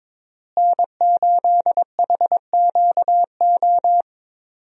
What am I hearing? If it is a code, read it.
Morse code, heard as a keyed tone, about 22 wpm, 700 Hz. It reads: N8HQO
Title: USA Call Signs - 5 Characters - Repeated 3x - 22wpm